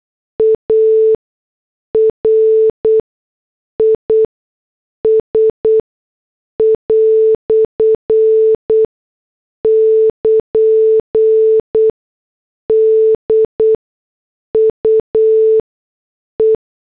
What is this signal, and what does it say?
Morse code, heard as a keyed tone, about 8 wpm, spaced as for 6 wpm, 440 Hz. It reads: ARIS" <KN> DUE
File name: morse / headlines